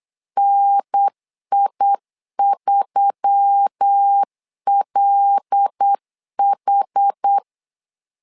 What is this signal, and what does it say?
Morse code, heard as a keyed tone, about 8 wpm, 800 Hz. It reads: NI3LH
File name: LEZIONI / 5 LBVC / 5 L 8wpm 2831